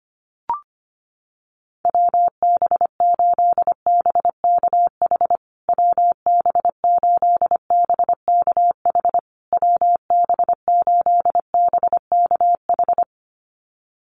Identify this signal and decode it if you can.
Morse code, heard as a keyed tone, about 25 wpm, 700 Hz. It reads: WB8BKH WB8BKH WB8BKH
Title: USA Call Signs Repeated 3x 25wpm